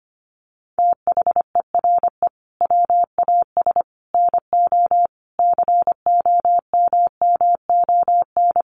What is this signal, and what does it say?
Morse code, heard as a keyed tone, about 25 wpm, 700 Hz. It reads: THERE WAS NO COMMON
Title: Sentences from Top 1200 Words 25wpm